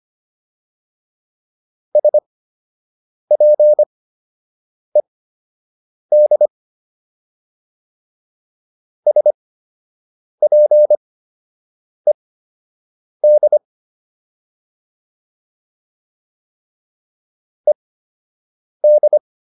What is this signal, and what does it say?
Morse code, heard as a keyed tone, about 25 wpm, spaced as for 7 wpm, 600 Hz. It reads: SPED SPED ED